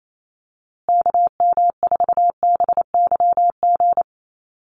Morse code, heard as a keyed tone, about 28 wpm, 700 Hz. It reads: KM4BYG
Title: USA Call Signs Repeated 3x 28wpm